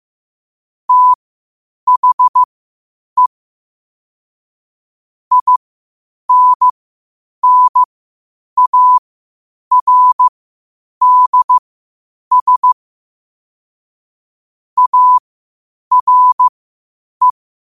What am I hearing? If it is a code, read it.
Morse code, heard as a keyed tone, about 15 wpm, spaced as for 8 wpm, 1000 Hz. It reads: THE INNARDS ARE
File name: qotdfiles / 8wpm